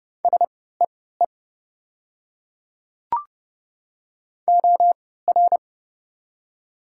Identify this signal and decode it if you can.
Morse code, heard as a keyed tone, about 30 wpm, spaced as for 17 wpm, 700 Hz. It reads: SEE OR
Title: 3x Character Spacing - Top 100 Words - 30wpm